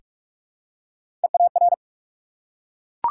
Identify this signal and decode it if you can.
Morse code, heard as a keyed tone, about 44 wpm, 700 Hz. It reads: EAR
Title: Three Letter Words - 45wpm